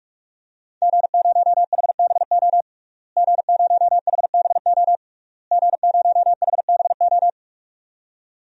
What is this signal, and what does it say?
Morse code, heard as a keyed tone, about 46 wpm, 700 Hz. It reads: G0HBO G0HBO G0HBO
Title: DXCC-100 Call Signs - 5 Characters - Repeated 3x - 45wpm